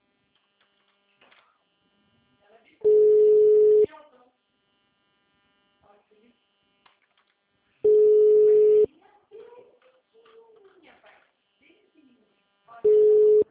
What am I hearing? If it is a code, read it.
Ringback tone (Europe, 425 Hz), with speech.